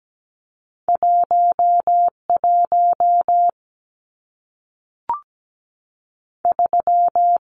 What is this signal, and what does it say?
Morse code, heard as a keyed tone, about 17 wpm, 700 Hz. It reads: 1 1 3
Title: Numbers - 2-Digit 17wpm